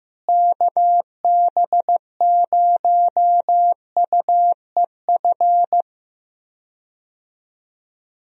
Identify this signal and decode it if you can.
Morse code, heard as a keyed tone, about 15 wpm, 700 Hz. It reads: KB0UEF
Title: USA Call Signs - 6 Characters - Repeated 2x - 15wpm